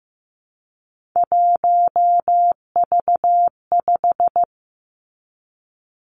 Morse code, heard as a keyed tone, about 15 wpm, 700 Hz. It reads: 1V5
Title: Three Challenging Letter-Number Combinations 15wpm